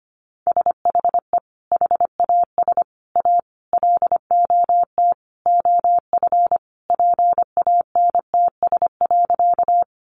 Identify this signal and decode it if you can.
Morse code, heard as a keyed tone, about 25 wpm, 700 Hz. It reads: SHE HAS A LOT OF PANTS.
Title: Additional Easy Phrases 25wpm